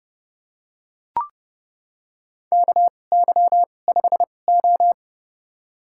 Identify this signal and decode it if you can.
Morse code, heard as a keyed tone, about 30 wpm, spaced as for 22 wpm, 700 Hz. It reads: KY5O
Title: USA Call Signs - 4 Characters - 2x Character Spacing - 30wpm